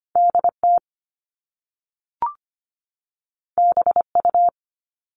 Morse code, heard as a keyed tone, about 25 wpm, 700 Hz. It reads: DT BU
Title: All Cut Numbers - 2-Digit 25wpm